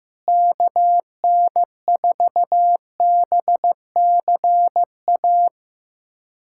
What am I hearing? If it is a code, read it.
Morse code, heard as a keyed tone, about 16 wpm, 700 Hz. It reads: KN4BCA
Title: USA Callsign DE Callsign - 6 Characters - 15wpm